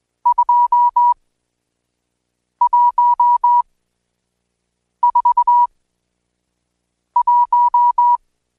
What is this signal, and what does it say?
Morse code, heard as a keyed tone, about 20 wpm, spaced as for 5 wpm, 970 Hz. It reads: 2141
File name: Code Quick / AudioDownld1 / Seg76-Day8-DayDrill 8